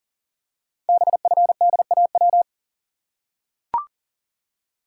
Morse code, heard as a keyed tone, about 38 wpm, 700 Hz. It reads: BFDAW
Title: Code Groups - 5 Letters - 40wpm